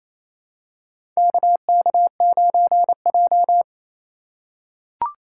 Morse code, heard as a keyed tone, about 28 wpm, 700 Hz. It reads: KK9J
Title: USA Call Signs - 4 Characters - 4x Character Spacing - 28wpm